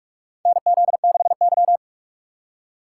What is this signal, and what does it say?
Morse code, heard as a keyed tone, about 41 wpm, 700 Hz. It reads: NZ6Y
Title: USA Call Signs - 4 Characters - 4x Character Spacing - 45wpm